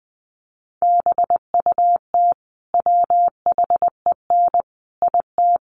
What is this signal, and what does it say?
Morse code, heard as a keyed tone, about 20 wpm, 700 Hz. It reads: BUT WHEN IT
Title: Sets of 5 Words 20wpm